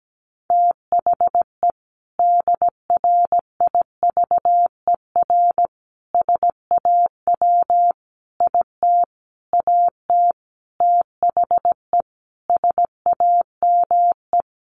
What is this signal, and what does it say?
Morse code, heard as a keyed tone, about 17 wpm, 700 Hz. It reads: THE DRIVER SAW IT AT THE SAME
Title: Sentences from Top 1000 Words 17wpm